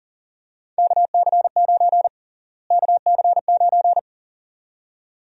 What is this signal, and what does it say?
Morse code, heard as a keyed tone, about 40 wpm, 700 Hz. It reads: KC9 KC9
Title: USA Call Sign Prefixes - Repeated 2x - 40wpm